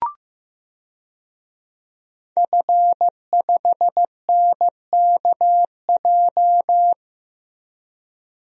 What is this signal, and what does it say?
Morse code, heard as a keyed tone, about 15 wpm, 700 Hz. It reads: F5NKJ
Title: DXCC-100 Call Signs - 5 Characters - Repeated 3x - 15wpm